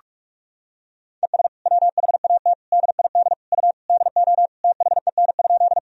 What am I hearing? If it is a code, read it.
Morse code, heard as a keyed tone, about 46 wpm, 700 Hz. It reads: ES WHAT DID U DO THEN?
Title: Sentences from Top 100 Words and Abbreviations 45wpm